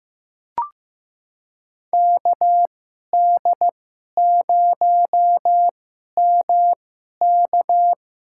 Morse code, heard as a keyed tone, about 15 wpm, spaced as for 11 wpm, 700 Hz. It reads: KD0MK
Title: USA Call Signs - 2x Character Spacing - 15wpm